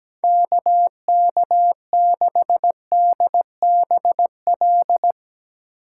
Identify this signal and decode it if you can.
Morse code, heard as a keyed tone, about 19 wpm, 700 Hz. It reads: KK6DBL
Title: Speed Racing - USA 2x3 Call Signs 17wpm